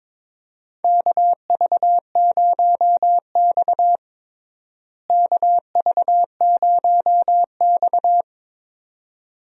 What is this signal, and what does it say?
Morse code, heard as a keyed tone, about 22 wpm, 700 Hz. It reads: KV0X KV0X
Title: USA Call Signs - 4 Characters - Repeated 2x - 22wpm